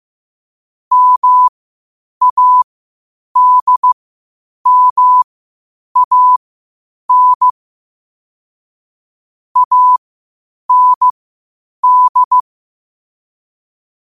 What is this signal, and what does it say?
Morse code, heard as a keyed tone, about 15 wpm, spaced as for 8 wpm, 1000 Hz. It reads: MADMAN AND
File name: qotdfiles / 8wpm